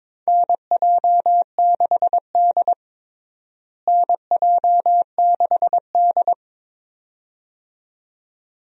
Morse code, heard as a keyed tone, about 22 wpm, 700 Hz. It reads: NJ6D NJ6D
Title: USA Call Signs - 4 Characters - Repeated 2x - 22wpm